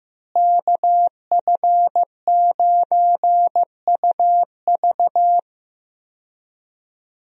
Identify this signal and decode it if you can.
Morse code, heard as a keyed tone, about 15 wpm, 700 Hz. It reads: KF9UV
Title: USA Call Signs - 5 Characters - Repeated 3x - 15wpm